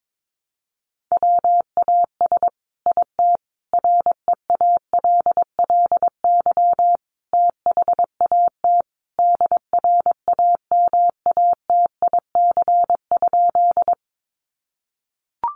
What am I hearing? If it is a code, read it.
Morse code, heard as a keyed tone, about 22 wpm, 700 Hz. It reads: WAS IT REALLY THAT DRAMATIC?
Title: Sentences from Top 1300 Words 22wpm